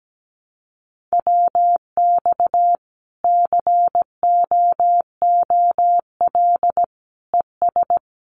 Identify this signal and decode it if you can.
Morse code, heard as a keyed tone, about 17 wpm, 700 Hz. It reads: WX COOL ES
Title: Common QSO Phrases 17wpm